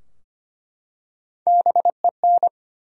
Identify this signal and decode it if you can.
Morse code, heard as a keyed tone, about 25 wpm, 700 Hz. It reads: BEN
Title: USA Common Names - Max 4 Letters - 25wpm